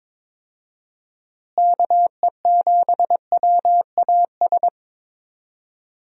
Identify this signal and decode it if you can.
Morse code, heard as a keyed tone, about 22 wpm, 700 Hz. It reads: KE7WAS
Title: USA Call Signs - 6 Characters - Repeated 3x - 22wpm